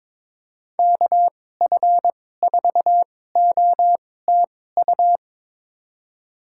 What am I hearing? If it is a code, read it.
Morse code, heard as a keyed tone, about 22 wpm, spaced as for 16 wpm, 700 Hz. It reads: KF4OTU
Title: USA Call Signs - 6 Characters - 2x Character Spacing - 22wpm